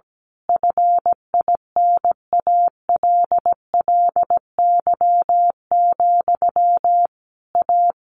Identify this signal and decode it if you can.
Morse code, heard as a keyed tone, about 17 wpm, 700 Hz. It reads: FINALLY, A
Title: A Sound Of Thunder 17wpm